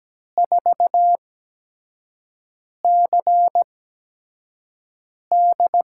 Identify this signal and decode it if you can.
Morse code, heard as a keyed tone, about 17 wpm, spaced as for 5 wpm, 700 Hz. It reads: 4CD
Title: USA Call Signs - 8x Character Spacing - 17wpm